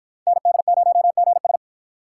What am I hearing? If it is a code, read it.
Morse code, heard as a keyed tone, about 53 wpm, 700 Hz. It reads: ND0GS